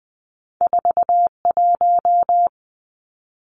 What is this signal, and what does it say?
Morse code, heard as a keyed tone, about 20 wpm, 700 Hz. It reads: 41
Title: Two Challenging Letter-Number Combinations 20wpm